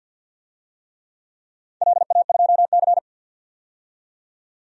Morse code, heard as a keyed tone, about 51 wpm, 700 Hz. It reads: LA2C
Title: DXCC-100 Call Signs - 4 Characters - 50wpm